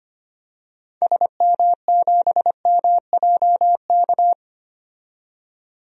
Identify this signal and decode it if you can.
Morse code, heard as a keyed tone, about 25 wpm, 700 Hz. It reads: SM7MJK
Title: DXCC-100 Call Signs - 6 Characters - 25wpm